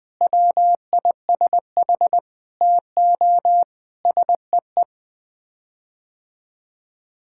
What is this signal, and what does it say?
Morse code, heard as a keyed tone, about 20 wpm, 700 Hz. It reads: WISH TO SEE
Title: Sets of 9 Words - 20wpm